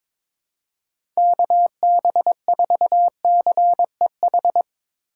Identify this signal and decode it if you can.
Morse code, heard as a keyed tone, about 22 wpm, 700 Hz. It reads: KB4CEH
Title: USA Call Signs - 6 Characters - 4x Character Spacing - 22wpm